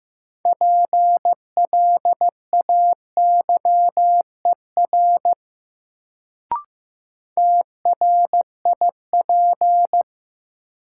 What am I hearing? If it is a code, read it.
Morse code, heard as a keyed tone, about 15 wpm, 700 Hz. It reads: PLAYER TRIP
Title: Sentences from Top 1000 Words - Review Part 4 - New Word Drill - 15wpm